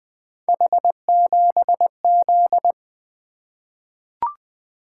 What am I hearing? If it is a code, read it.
Morse code, heard as a keyed tone, about 20 wpm, 700 Hz. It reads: H7Z
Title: Three Letter-Number Combinations - 20wpm